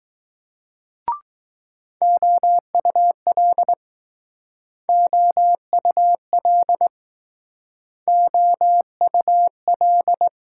Morse code, heard as a keyed tone, about 20 wpm, 700 Hz. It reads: OUL OUL OUL